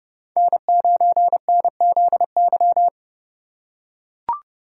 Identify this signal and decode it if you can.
Morse code, heard as a keyed tone, about 30 wpm, 700 Hz. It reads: N9NZY
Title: USA Call Signs - 5 Characters - 3x Character Spacing - 30wpm